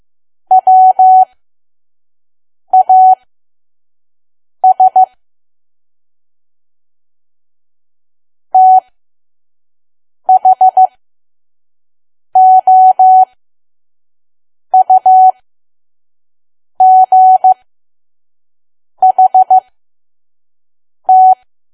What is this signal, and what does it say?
Morse code, heard as a keyed tone, about 15 wpm, spaced as for 5 wpm, 750 Hz. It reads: WAS THOUGHT
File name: Archive / 5 WPM files / PM